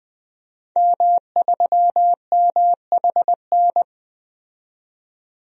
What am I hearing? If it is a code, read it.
Morse code, heard as a keyed tone, about 21 wpm, 700 Hz. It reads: M3MHN
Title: DXCC-100 Call Signs - 5 Characters - 20wpm